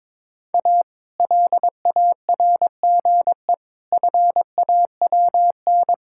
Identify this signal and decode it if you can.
Morse code, heard as a keyed tone, about 22 wpm, 700 Hz. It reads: A LARGE FAWN
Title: Pangram Challenge v2 - 22wpm